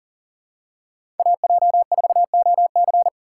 Morse code, heard as a keyed tone, about 40 wpm, 700 Hz. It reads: AJ4OC
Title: USA Call Signs - 5 Characters - 2x Character Spacing - 40wpm